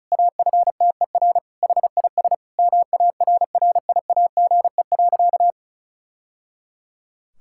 Morse code, heard as a keyed tone, about 34 wpm, 700 Hz. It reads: AFTER HIS MARRIAGE.
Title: Sentences from Top 1000 Words - Review Part 5 - 35wpm